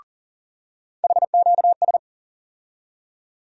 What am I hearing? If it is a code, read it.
Morse code, heard as a keyed tone, about 37 wpm, 700 Hz. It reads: HQS
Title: USA Call Sign Postfixes - 3 Letters 40wpm